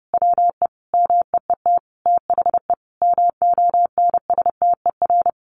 Morse code, heard as a keyed tone, about 30 wpm, 700 Hz. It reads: WE MEET THE MONSTER
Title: A Sound Of Thunder 30wpm